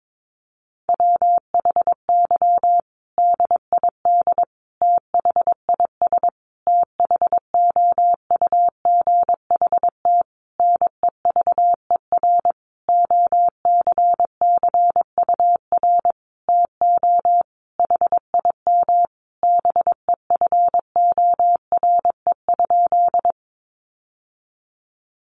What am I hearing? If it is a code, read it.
Morse code, heard as a keyed tone, about 22 wpm, 700 Hz. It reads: WHY DID THIS THOUGHT NEVER OCCUR TO HIM BEFORE?
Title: Sentences from Top 900 Words 22wpm